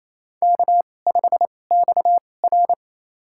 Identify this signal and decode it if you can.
Morse code, heard as a keyed tone, about 28 wpm, spaced as for 20 wpm, 700 Hz. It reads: K5XR
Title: USA Call Signs - 4 Characters - 2x Character Spacing - 28wpm